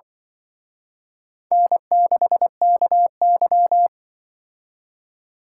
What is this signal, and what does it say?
Morse code, heard as a keyed tone, about 25 wpm, 700 Hz. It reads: N6KY